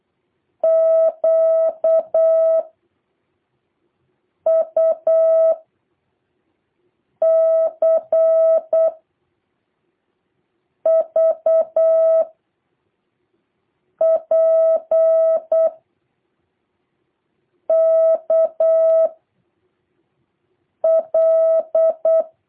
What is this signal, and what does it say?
Morse code, heard as a keyed tone, about 8 wpm, 650 Hz. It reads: Q U C V P K L
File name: jh8cbh / 2023cw